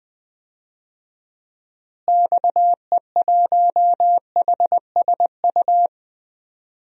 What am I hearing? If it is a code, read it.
Morse code, heard as a keyed tone, about 21 wpm, 700 Hz. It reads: XE1HSU
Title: DXCC-100 Call Signs - 6 Characters - Repeated 3x - 20wpm